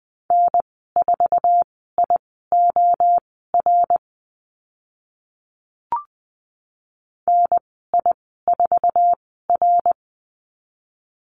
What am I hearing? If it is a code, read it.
Morse code, heard as a keyed tone, about 20 wpm, spaced as for 14 wpm, 700 Hz. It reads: N4IOR NI4R
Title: 069 Lesson 12g - Letter d - Call Signs 20wpm